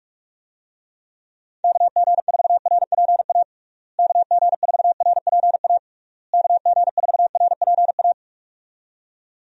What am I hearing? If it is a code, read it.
Morse code, heard as a keyed tone, about 41 wpm, 700 Hz. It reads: KG4RPA KG4RPA KG4RPA
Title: USA Call Signs - 6 Characters - Repeated 3x - 45wpm